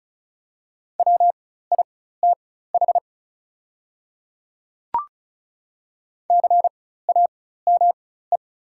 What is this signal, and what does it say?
Morse code, heard as a keyed tone, about 36 wpm, spaced as for 16 wpm, 700 Hz. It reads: WITH CAME
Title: 4x Character Spacing - Top 200 Words - 35wpm